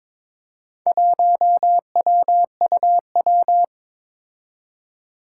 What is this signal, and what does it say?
Morse code, heard as a keyed tone, about 23 wpm, 700 Hz. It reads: 1WUW
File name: practice / 22wpm / Four Letter-Number Combinations - 22wpm